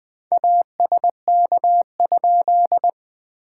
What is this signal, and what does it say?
Morse code, heard as a keyed tone, about 20 wpm, 700 Hz. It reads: ASK?
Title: SOTA QSO Practice 20wpm